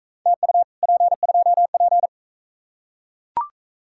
Morse code, heard as a keyed tone, about 41 wpm, 700 Hz. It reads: TU P2P